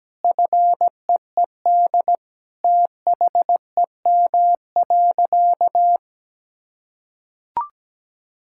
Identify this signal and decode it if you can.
Morse code, heard as a keyed tone, about 17 wpm, 700 Hz. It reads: FEED THEM.
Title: Sentences from Top 1600 Words 17wpm